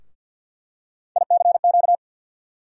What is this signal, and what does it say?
Morse code, heard as a keyed tone, about 47 wpm, 700 Hz. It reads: ICQ
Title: USA Call Sign Postfixes - 3 Letters 50wpm